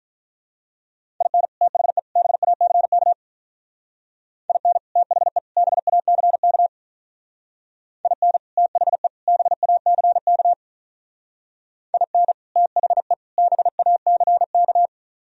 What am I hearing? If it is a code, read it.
Morse code, heard as a keyed tone, about 43 wpm, 700 Hz. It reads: IN THE BACK IN THE BACK IN THE BACK IN THE BACK
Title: Speed-Racing - Sets of 3 Words 35wpm